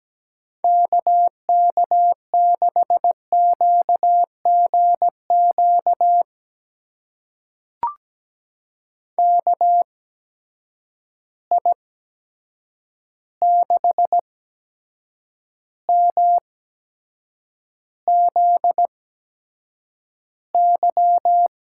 Morse code, heard as a keyed tone, about 17 wpm, spaced as for 7 wpm, 700 Hz. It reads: KK6QGQ KI6MZY
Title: USA Call Signs - 6 Characters - 8x Character Spacing - 17wpm